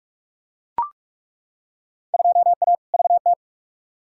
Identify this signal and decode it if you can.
Morse code, heard as a keyed tone, about 41 wpm, 700 Hz. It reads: 2A VT
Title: Field Day Exchanges Encore - 45wpm